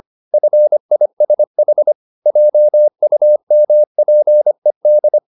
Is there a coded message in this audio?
Yes: Morse code from 0.3 to 5.2 s, heard as a keyed tone, about 25 wpm, 600 Hz. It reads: FISH JUMPED